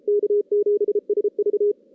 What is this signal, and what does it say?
Morse code, heard as a keyed tone, about 29 wpm, 410 Hz. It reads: K7SV
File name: 2016 JAN CW NAQP / 2016 Jan naqp k5ot - 0-6